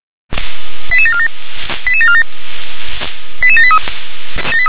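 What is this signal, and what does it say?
Five-tone selective call (ZVEI), heard as quick runs of five tones, tones of about 70 ms. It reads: 88747 88747 88783
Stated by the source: ZVEI five-tone selective call received via SDR.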